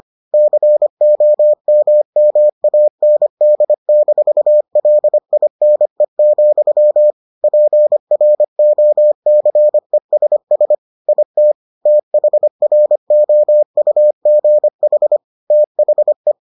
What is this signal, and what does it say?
Morse code, heard as a keyed tone, about 25 wpm, 600 Hz. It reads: COMMAND-LINE, PROCESS IT THROUGH THE